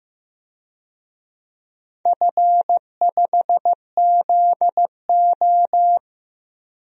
Morse code, heard as a keyed tone, about 15 wpm, 700 Hz. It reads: F5ZO
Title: DXCC-100 Call Signs - 4 Characters - 15wpm